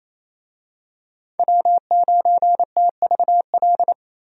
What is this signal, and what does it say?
Morse code, heard as a keyed tone, about 28 wpm, 700 Hz. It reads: W9TVL
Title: USA Call Signs - 5 Characters - Repeated 2x - 28wpm